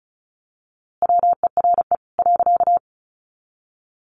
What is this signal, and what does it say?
Morse code, heard as a keyed tone, about 34 wpm, 700 Hz. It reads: WERE .
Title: A Sound Of Thunder 35wpm